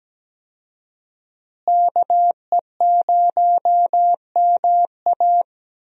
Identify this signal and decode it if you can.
Morse code, heard as a keyed tone, about 17 wpm, 700 Hz. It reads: KE0MA
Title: USA Call Signs - 5 Characters - Repeated 3x - 17wpm